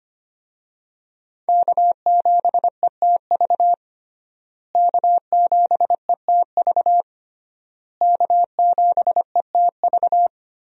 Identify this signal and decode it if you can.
Morse code, heard as a keyed tone, about 25 wpm, 700 Hz. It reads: K7ETV K7ETV K7ETV
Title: USA Call Signs - 5 Characters - Repeated 3x - 25wpm